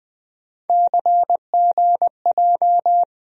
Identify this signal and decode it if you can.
Morse code, heard as a keyed tone, about 20 wpm, 700 Hz. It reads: CGJ
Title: USA Call Sign Postfixes - 3 Letters 20wpm